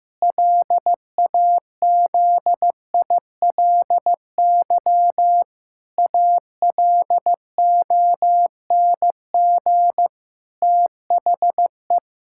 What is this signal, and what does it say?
Morse code, heard as a keyed tone, about 16 wpm, 700 Hz. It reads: LAZILY ALONG THE
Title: Pangram Challenge v2 - 15wpm